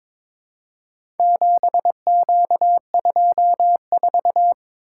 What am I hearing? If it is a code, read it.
Morse code, heard as a keyed tone, about 22 wpm, 700 Hz. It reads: 7Q24
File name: practice / 22wpm / Four Letter-Number Combinations - 22wpm